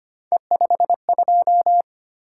Morse code, heard as a keyed tone, about 25 wpm, 700 Hz. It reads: E52
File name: Three Letter-Number Combinations - 25wpm